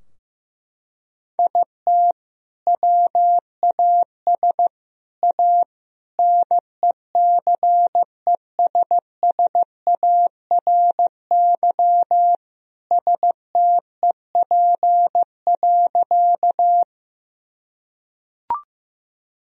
Morse code, heard as a keyed tone, about 15 wpm, 700 Hz. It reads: IT WAS A NECESSARY STEP.
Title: Sentences from Top 1000 Words - Review Part 2 - 15wpm